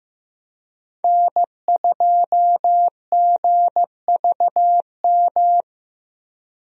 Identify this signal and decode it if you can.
Morse code, heard as a keyed tone, about 15 wpm, 700 Hz. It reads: N2GVM
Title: USA Call Signs - 5 Characters - Repeated 2x - 15wpm